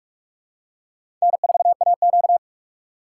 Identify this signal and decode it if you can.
Morse code, heard as a keyed tone, about 45 wpm, 700 Hz. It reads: N4AQ
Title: USA Call Signs - 4 Characters - 4x Character Spacing - 45wpm